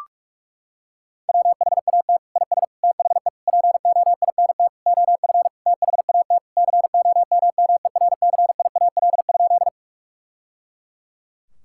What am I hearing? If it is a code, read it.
Morse code, heard as a keyed tone, about 41 wpm, 700 Hz. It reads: WHAT IS THE POINT OF THAT COMMERCIAL?
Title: Sentences from Top 1000 Words - Review Part 5 - 45wpm